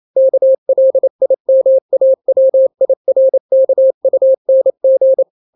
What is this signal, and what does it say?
Morse code, heard as a keyed tone, about 28 wpm, 530 Hz. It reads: KLIMAWIRKUNG
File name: daily / 27wpm